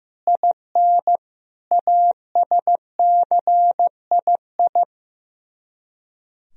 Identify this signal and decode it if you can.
Morse code, heard as a keyed tone, about 15 wpm, 700 Hz. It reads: IN ASCII
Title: Speed-Racing - Sets of 5 Words 15wpm